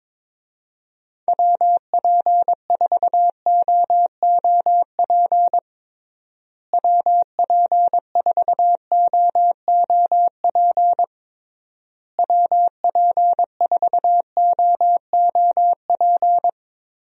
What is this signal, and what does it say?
Morse code, heard as a keyed tone, about 22 wpm, 700 Hz. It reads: WP4OOP WP4OOP WP4OOP
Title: USA Call Signs - 6 Characters - Repeated 3x - 22wpm